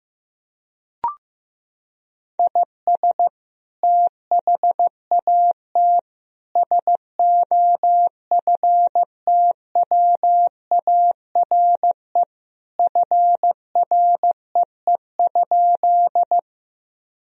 Morse code, heard as a keyed tone, about 15 wpm, 700 Hz. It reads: IS THAT SOFTWARE FREE?
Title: Sentences from Top 1000 Words - Review Part 3 - 15wpm